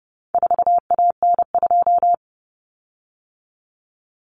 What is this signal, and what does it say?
Morse code, heard as a keyed tone, about 30 wpm, 700 Hz. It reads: 4AN2
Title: Common Cut Numbers - 4-Digit 30wpm